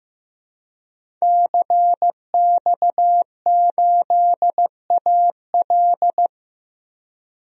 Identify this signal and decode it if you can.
Morse code, heard as a keyed tone, about 15 wpm, 700 Hz. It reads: CX8AL
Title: DXCC-100 Call Signs - 5 Characters - 15wpm